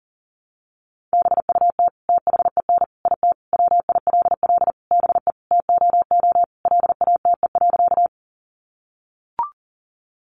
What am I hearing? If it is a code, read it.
Morse code, heard as a keyed tone, about 40 wpm, 700 Hz. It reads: BUT THEN IT WILL BE TOO LATE.
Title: Sentences from Top 300 Words 40wpm